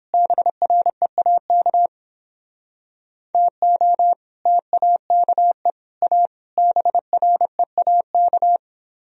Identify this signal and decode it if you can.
Morse code, heard as a keyed tone, about 31 wpm, 700 Hz. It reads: BREAK TO TAKE A BREAK